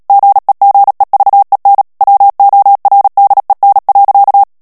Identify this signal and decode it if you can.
Morse code, heard as a keyed tone, about 37 wpm, 800 Hz. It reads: GEGEVEN WORDEN.
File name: morse mp3 / HOLLANDAIS texte clair 2 vitesse 35 mots minute